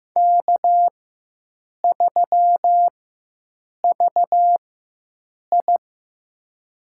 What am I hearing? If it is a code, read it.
Morse code, heard as a keyed tone, about 15 wpm, spaced as for 7 wpm, 700 Hz. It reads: K3VI
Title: USA Call Signs - 4x Character Spacing - 15wpm